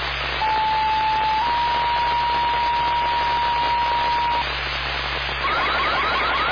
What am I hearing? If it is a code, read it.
Two-tone page, heard as a one-second tone, then a three-second tone, about 870 then 950 Hz, with noise. It is heard from 0.4 to 4.4 s.
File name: Indian Lake Fire